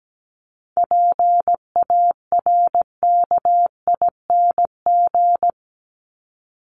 Morse code, heard as a keyed tone, about 17 wpm, 700 Hz. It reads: PARKING
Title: Top 1501-1750 Words 17wpm